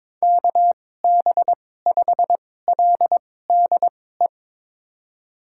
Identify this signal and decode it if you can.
Morse code, heard as a keyed tone, about 22 wpm, spaced as for 16 wpm, 700 Hz. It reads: KB5LDE
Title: USA Call Signs - 6 Characters - 2x Character Spacing - 22wpm